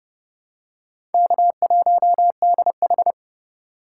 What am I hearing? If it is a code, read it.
Morse code, heard as a keyed tone, about 30 wpm, 700 Hz. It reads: K1DH
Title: USA Call Signs - 4 Characters - 3x Character Spacing - 30wpm